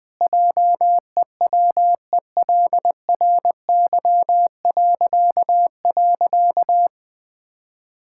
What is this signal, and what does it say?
Morse code, heard as a keyed tone, about 21 wpm, 700 Hz. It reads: JEWELRY..
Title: Pangram Challenge v2 - 20wpm